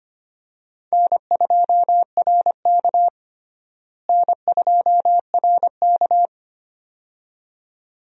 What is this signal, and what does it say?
Morse code, heard as a keyed tone, about 25 wpm, 700 Hz. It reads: N2RK N2RK
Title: USA Call Signs - 4 Characters - Repeated 2x - 25wpm